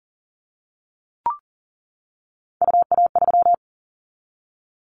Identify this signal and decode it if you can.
Morse code, heard as a keyed tone, about 42 wpm, 700 Hz. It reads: UA3
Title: Callsign Prefixes for Top 25 DXCC - 40wpm